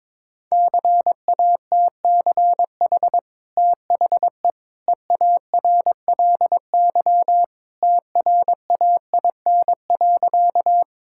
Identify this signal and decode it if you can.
Morse code, heard as a keyed tone, about 22 wpm, 690 Hz. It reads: CATCH THE EARLY TRAIN.
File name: practice / 22wpm / Easy Sentences with Hints - 22wpm